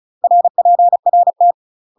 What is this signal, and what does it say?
Morse code, heard as a keyed tone, about 36 wpm, 700 Hz. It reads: RPRT
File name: UK MASTER QSOs TRAINING / 35WPM / UK MASTER QSOs 0011